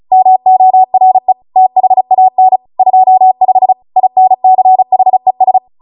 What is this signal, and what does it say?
Morse code, heard as a keyed tone, about 36 wpm, 750 Hz. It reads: MORE THAN 25 INCHES
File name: Morse / 35WPM